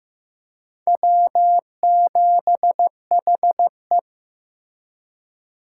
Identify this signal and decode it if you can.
Morse code, heard as a keyed tone, about 15 wpm, 700 Hz. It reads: W7HE
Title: USA Call Signs - 4 Characters - 2x Character Spacing - 15wpm